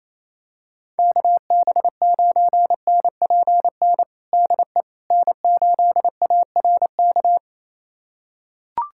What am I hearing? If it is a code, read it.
Morse code, heard as a keyed tone, about 28 wpm, 700 Hz. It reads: KB9NPN DE N8ARK
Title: USA Callsign DE Callsign 28wpm